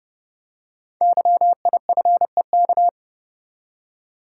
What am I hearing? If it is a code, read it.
Morse code, heard as a keyed tone, about 30 wpm, 700 Hz. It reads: YIFEK
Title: Code Groups - 5 Letters - 30wpm